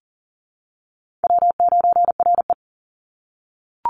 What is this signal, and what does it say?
Morse code, heard as a keyed tone, about 40 wpm, 700 Hz. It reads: W9RE
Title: Easy Call Signs - 40wpm